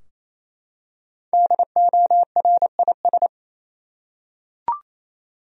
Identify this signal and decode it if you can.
Morse code, heard as a keyed tone, about 27 wpm, 700 Hz. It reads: DORIS
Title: USA Common Names - Max 5 Letters - 28wpm